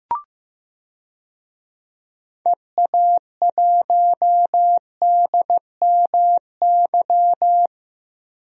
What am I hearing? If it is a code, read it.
Morse code, heard as a keyed tone, about 15 wpm, 700 Hz. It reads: EA1DMY
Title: DXCC-100 Call Signs - 6 Characters - 15wpm